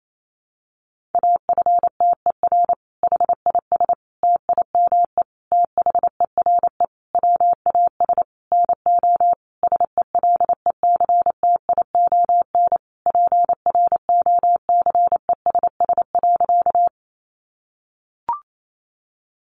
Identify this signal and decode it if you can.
Morse code, heard as a keyed tone, about 28 wpm, 700 Hz. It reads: AFTER HIS TIME THERE WAS NO SELECTION PROCESS.
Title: Sentences from Top 1400 Words 28wpm